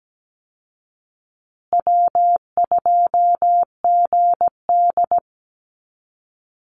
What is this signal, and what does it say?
Morse code, heard as a keyed tone, about 17 wpm, 700 Hz. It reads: W2GD
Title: Easy Call Signs Repeated 3x - 17wpm